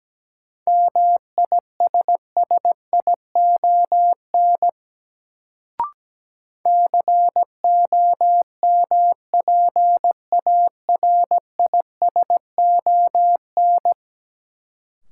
Morse code, heard as a keyed tone, about 17 wpm, 700 Hz. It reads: MISSION COMPARISON
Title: Sentences from Top 1600 Words - New Word Drill Alt - 17wpm